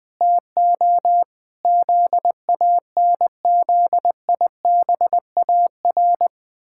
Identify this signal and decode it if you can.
Morse code, heard as a keyed tone, about 20 wpm, 700 Hz. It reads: TO ZANZIBAR
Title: Pangram Challenge v2 - 20wpm